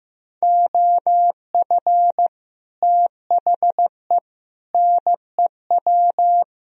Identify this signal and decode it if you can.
Morse code, heard as a keyed tone, about 16 wpm, 700 Hz. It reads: OF THE NEW
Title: Speed-Racing - Sets of 5 Words 15wpm